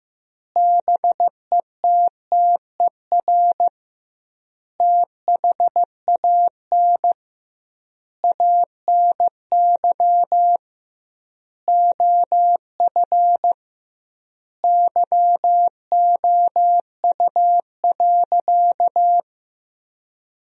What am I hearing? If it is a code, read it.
Morse code, heard as a keyed tone, about 15 wpm, 700 Hz. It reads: BETTER THAN ANY OF YOU.
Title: 2x Word-Spacing - Sentences from Top 400 Words - 15wpm